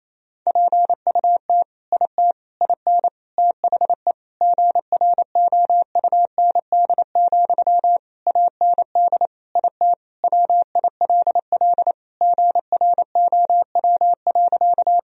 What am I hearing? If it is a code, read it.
Morse code, heard as a keyed tone, about 28 wpm, 700 Hz. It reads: PUT IT IN THE GROUND, AND IT WILL GROW.
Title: Sentences from Top 1000 Words - Review Part 2 - 28wpm